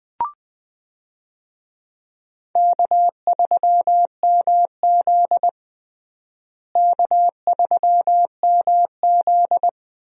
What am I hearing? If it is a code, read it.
Morse code, heard as a keyed tone, about 20 wpm, 700 Hz. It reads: K3MZ K3MZ
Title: USA Call Signs - 4 Characters - Repeated 2x - 20wpm